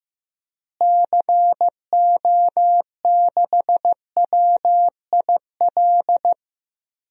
Morse code, heard as a keyed tone, about 15 wpm, 700 Hz. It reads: CO6WIL
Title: DXCC-100 Call Signs - 6 Characters - Repeated 2x - 15wpm